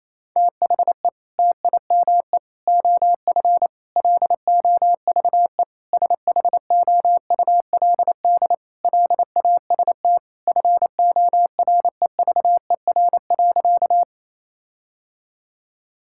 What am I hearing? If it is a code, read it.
Morse code, heard as a keyed tone, about 28 wpm, 700 Hz. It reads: THE TIME OF LOVE SHOULD LAST FOREVER.
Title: Sentences from Top 1600 Words 28wpm